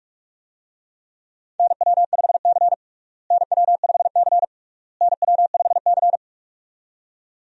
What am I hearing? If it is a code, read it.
Morse code, heard as a keyed tone, about 46 wpm, 700 Hz. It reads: NW5C NW5C NW5C
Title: USA Call Signs - 4 Characters - Repeated 3x - 45wpm